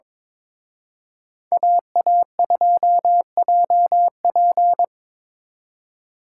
Morse code, heard as a keyed tone, about 25 wpm, 700 Hz. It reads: AA2JP
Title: Speed Racing - USA 2x2 Call Signs 22wpm